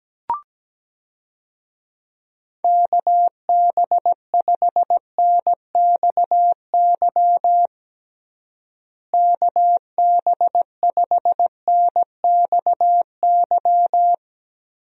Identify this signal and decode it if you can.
Morse code, heard as a keyed tone, about 17 wpm, 700 Hz. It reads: KB5NXY KB5NXY
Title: USA Call Signs - 6 Characters - Repeated 3x - 17wpm